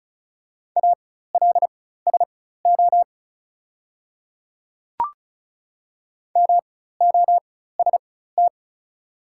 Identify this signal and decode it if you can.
Morse code, heard as a keyed tone, about 36 wpm, spaced as for 16 wpm, 700 Hz. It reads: ALSO MOST
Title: 4x Character Spacing - Top 200 Words - 35wpm